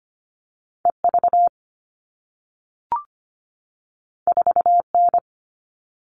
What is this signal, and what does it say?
Morse code, heard as a keyed tone, about 25 wpm, 700 Hz. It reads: E V 4 N
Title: All Cut Numbers - 2-Digit 25wpm